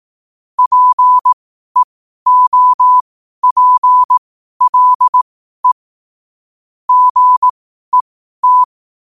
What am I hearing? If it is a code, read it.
Morse code, heard as a keyed tone, about 18 wpm, spaced as for 12 wpm, 1000 Hz. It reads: PEOPLE GET
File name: qotdfiles / 12wpm